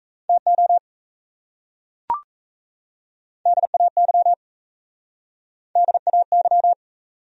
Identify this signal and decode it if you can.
Morse code, heard as a keyed tone, about 37 wpm, 700 Hz. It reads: TO DAY DAY